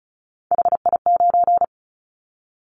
Morse code, heard as a keyed tone, about 34 wpm, 700 Hz. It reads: HI9
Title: Callsign Prefixes for Top 100 DXCC - 35wpm